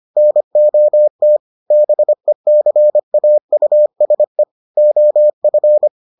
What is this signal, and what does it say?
Morse code, heard as a keyed tone, about 25 wpm, 600 Hz. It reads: NOT BECAUSE OF